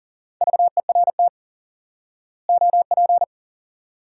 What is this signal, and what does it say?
Morse code, heard as a keyed tone, about 40 wpm, 700 Hz. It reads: VERT OP
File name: IQE Course 3 - Lesson 89 - YAGI - Two Elements - 40wpm